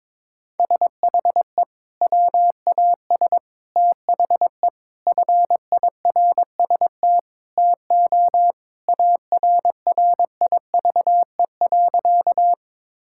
Morse code, heard as a keyed tone, about 21 wpm, 700 Hz. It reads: SHE WAS THE FIRST TO ARRIVE.
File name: practice / 22wpm / Sentences with 3-Word Phrases - 22wpm